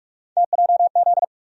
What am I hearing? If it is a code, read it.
Morse code, heard as a keyed tone, about 42 wpm, 700 Hz. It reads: TJZ
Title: USA Call Sign Postfixes - 3 Letters 45wpm